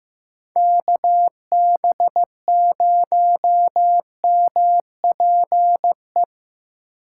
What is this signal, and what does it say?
Morse code, heard as a keyed tone, about 15 wpm, 700 Hz. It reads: KB0MPE
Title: USA Call Signs - 6 Characters - 8x Character Spacing - 15wpm